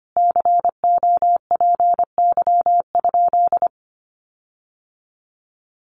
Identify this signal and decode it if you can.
Morse code, heard as a keyed tone, about 25 wpm, 700 Hz. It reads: COPY?
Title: Sentences from Top 1400 Words 25wpm